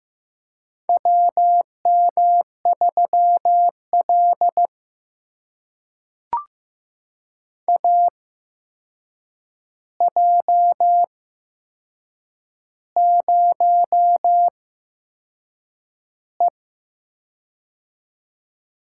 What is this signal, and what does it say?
Morse code, heard as a keyed tone, about 15 wpm, spaced as for 6 wpm, 700 Hz. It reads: WM3L AJ0E
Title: USA Call Signs - 4 Characters - 8x Character Spacing - 15wpm